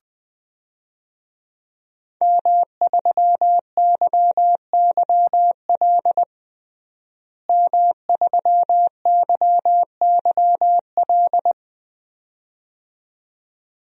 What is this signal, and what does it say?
Morse code, heard as a keyed tone, about 20 wpm, 700 Hz. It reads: M3YYL M3YYL
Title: DXCC-100 Call Signs - 5 Characters - Repeated 2x - 20wpm